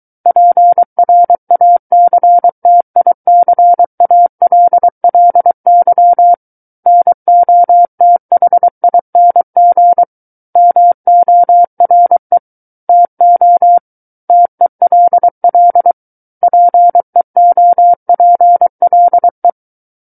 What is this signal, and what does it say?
Morse code, heard as a keyed tone, about 23 wpm, 700 Hz. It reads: PRACTICALLY NOTHING MORE TO TELL PEOPLE
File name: Book One / 14-Chapter 14 - 23 WPM